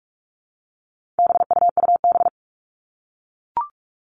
Morse code, heard as a keyed tone, about 44 wpm, 700 Hz. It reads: BUUB
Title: All Cut Numbers - 4-Digit 45wpm